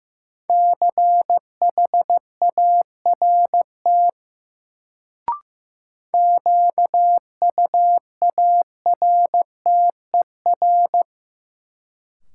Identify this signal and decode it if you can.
Morse code, heard as a keyed tone, about 15 wpm, 700 Hz. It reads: CHART QUARTER
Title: Sentences from Top 1300 Words - New Word Drill Alt - 15wpm